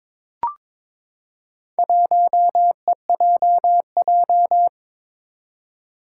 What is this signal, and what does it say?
Morse code, heard as a keyed tone, about 22 wpm, 700 Hz. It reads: 1EJJ
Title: Four Challenging Letter-Number Combinations 22wpm